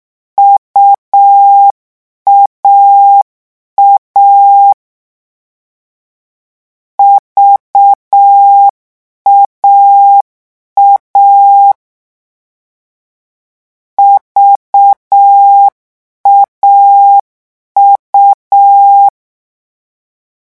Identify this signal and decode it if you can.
Morse code, heard as a keyed tone, about 6 wpm, 800 Hz. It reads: UAA VAA VAU
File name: morse mp3 / lecon3-vitesse 6